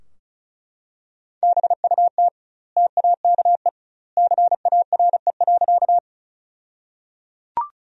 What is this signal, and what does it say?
Morse code, heard as a keyed tone, about 34 wpm, 700 Hz. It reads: BUT TAKE CARE.
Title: 2x Word-Spacing - Sentences from Top 300 Words - 35wpm